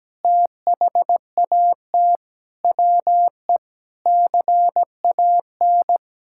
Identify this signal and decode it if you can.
Morse code, heard as a keyed tone, about 17 wpm, 700 Hz. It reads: THAT WE CAN
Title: Sentences from Top 1000 Words - Review Part 5 - 17wpm